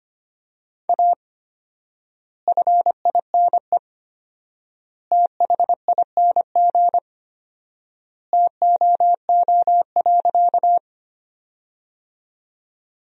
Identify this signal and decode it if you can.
Morse code, heard as a keyed tone, about 25 wpm, 700 Hz. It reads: A FINE THING TOO.
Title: 4x Word-Spacing - Sentences from Top 500 Words - 25wpm